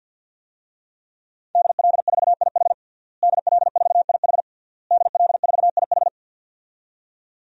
Morse code, heard as a keyed tone, about 47 wpm, 700 Hz. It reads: DL4IH DL4IH DL4IH
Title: DXCC-100 Call Signs - 5 Characters - Repeated 3x - 50wpm